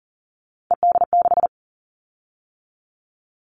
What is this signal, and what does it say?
Morse code, heard as a keyed tone, about 40 wpm, 700 Hz. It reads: ED6
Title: Callsign Prefixes for Top 100 DXCC - 40wpm